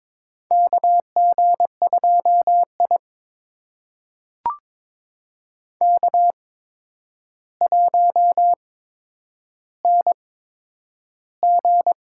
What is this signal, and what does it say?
Morse code, heard as a keyed tone, about 22 wpm, spaced as for 9 wpm, 700 Hz. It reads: KG2I K1NG
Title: USA Call Signs - 4 Characters - 8x Character Spacing - 22wpm